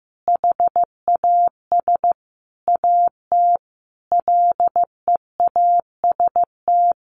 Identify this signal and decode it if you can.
Morse code, heard as a keyed tone, about 14 wpm, 700 Hz. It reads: HAS AT LEAST
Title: Sentences from Top 1500 Words 15wpm